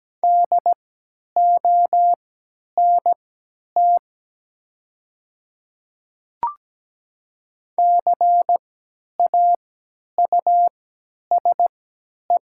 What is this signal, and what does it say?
Morse code, heard as a keyed tone, about 17 wpm, spaced as for 10 wpm, 700 Hz. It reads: DONT CAUSE
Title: 3x Character Spacing - Top 300 Words - 17wpm